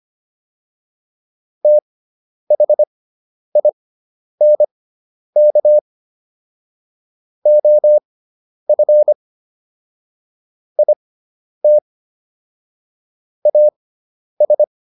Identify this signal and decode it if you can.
Morse code, heard as a keyed tone, about 25 wpm, spaced as for 10 wpm, 600 Hz. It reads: THINK OF IT AS